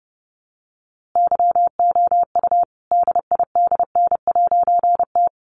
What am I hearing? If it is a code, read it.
Morse code, heard as a keyed tone, about 30 wpm, 700 Hz. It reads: YOU DIDN'T
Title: A Sound Of Thunder 30wpm